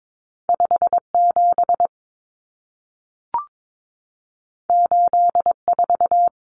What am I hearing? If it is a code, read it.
Morse code, heard as a keyed tone, about 22 wpm, 700 Hz. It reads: 57 84
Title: Common Cut Numbers - 2-Digit 22wpm